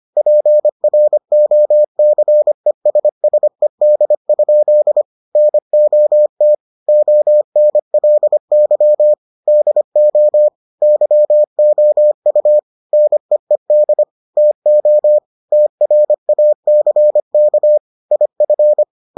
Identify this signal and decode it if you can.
Morse code, heard as a keyed tone, about 25 wpm, 600 Hz. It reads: PROCESSED? NOT ONLY DO YOU NEED TO TRACK IF